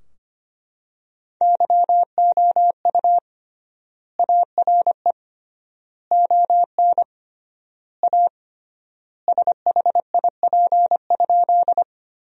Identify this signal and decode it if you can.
Morse code, heard as a keyed tone, about 25 wpm, 700 Hz. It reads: YOU ARE ON A SHIP?
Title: 3x Word-Spacing - Sentences from Top 400 Words - 25wpm